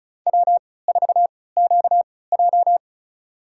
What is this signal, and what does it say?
Morse code, heard as a keyed tone, about 36 wpm, spaced as for 20 wpm, 700 Hz. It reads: W4QJ
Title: USA Call Signs - 3x Character Spacing - 35wpm